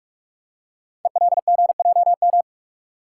Morse code, heard as a keyed tone, about 44 wpm, 700 Hz. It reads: ELGJM
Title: Code Groups - 5 Letters - 45wpm